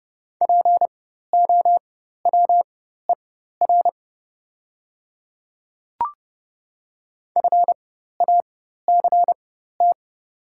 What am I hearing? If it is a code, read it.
Morse code, heard as a keyed tone, about 31 wpm, spaced as for 14 wpm, 700 Hz. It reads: POWER FACT
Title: 4x Character Spacing - Top 500 Words - 30wpm